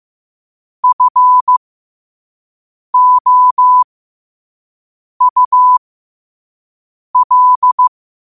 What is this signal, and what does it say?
Morse code, heard as a keyed tone, about 15 wpm, spaced as for 5 wpm, 1000 Hz. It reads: FOUL
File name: qotdfiles / 5wpm